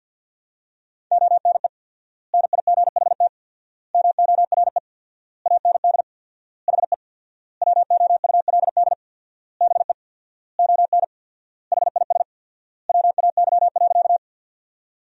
Morse code, heard as a keyed tone, about 46 wpm, 700 Hz. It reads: ONE NIGHT MORE AND HE WOULD BE ON HIS WAY.
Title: 4x Word-Spacing - Sentences from Top 300 Words - 50wpm